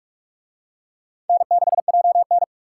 Morse code, heard as a keyed tone, about 42 wpm, 700 Hz. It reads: N6JN